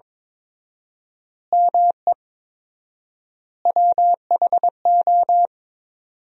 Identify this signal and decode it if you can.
Morse code, heard as a keyed tone, about 22 wpm, 700 Hz. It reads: ME WHO
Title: 4x Word-Spacing - Sentences from Top 200 Words - 22wpm